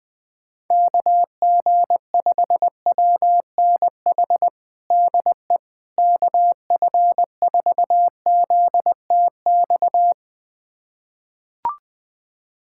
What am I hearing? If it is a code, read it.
Morse code, heard as a keyed tone, about 21 wpm, 700 Hz. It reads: KG5WNH DE KF4ZTX
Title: USA Callsign DE Callsign 20wpm